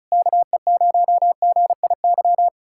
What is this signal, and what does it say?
Morse code, heard as a keyed tone, about 40 wpm, 700 Hz. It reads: KE0GIY